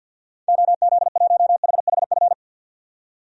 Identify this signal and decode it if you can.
Morse code, heard as a keyed tone, about 52 wpm, 700 Hz. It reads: OZ1HHF
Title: DXCC-100 Call Signs - 6 Characters - 50wpm